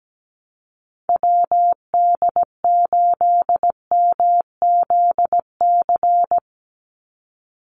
Morse code, heard as a keyed tone, about 17 wpm, 700 Hz. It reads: WD8MZC
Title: USA Call Signs 17wpm